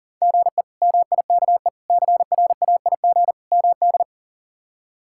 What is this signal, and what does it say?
Morse code, heard as a keyed tone, about 40 wpm, 700 Hz. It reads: GE MIKE CRAIG MD
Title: K1USN SST QSO Practice - 40wpm